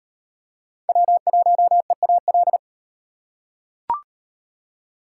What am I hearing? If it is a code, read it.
Morse code, heard as a keyed tone, about 36 wpm, 700 Hz. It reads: W1EAL